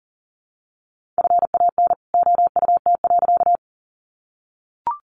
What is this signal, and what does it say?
Morse code, heard as a keyed tone, about 40 wpm, 700 Hz. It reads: FAN OUT.
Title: Sentences from Top 1300 Words 40wpm